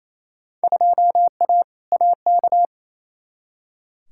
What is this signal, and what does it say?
Morse code, heard as a keyed tone, about 28 wpm, 700 Hz. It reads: 2A AK
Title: Field Day Exchanges Encore - 28wpm